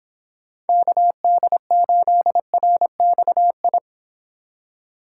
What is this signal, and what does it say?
Morse code, heard as a keyed tone, about 26 wpm, 700 Hz. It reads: KD8RXI